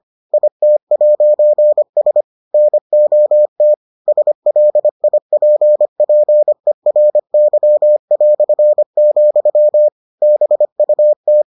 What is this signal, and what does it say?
Morse code, heard as a keyed tone, about 25 wpm, 600 Hz. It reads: IT'S NOT SLIPPERY", BUT